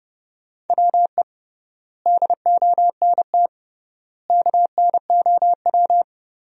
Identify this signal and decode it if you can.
Morse code, heard as a keyed tone, about 29 wpm, 700 Hz. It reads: WE DONT KNOW
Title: 3x Word-Spacing - Sentences from Top 400 Words - 30wpm